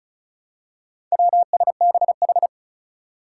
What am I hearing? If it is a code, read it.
Morse code, heard as a keyed tone, about 36 wpm, 700 Hz. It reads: WSBH
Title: Four Challenging Letter-Number Combinations 35wpm